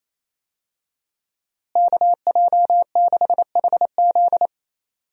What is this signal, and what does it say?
Morse code, heard as a keyed tone, about 28 wpm, 700 Hz. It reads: KJ6HZ
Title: USA Call Signs - 5 Characters - Repeated 2x - 28wpm